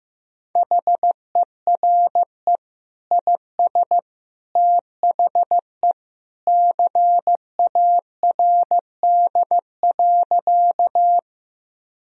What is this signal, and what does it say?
Morse code, heard as a keyed tone, about 15 wpm, 700 Hz. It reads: HERE IS THE CARD.
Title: Sentences from Top 1000 Words - Review Part 4 - 15wpm